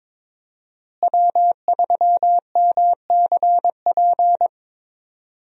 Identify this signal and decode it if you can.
Morse code, heard as a keyed tone, about 22 wpm, 700 Hz. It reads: W3MCP
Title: USA Call Signs - 5 Characters - Repeated 3x - 22wpm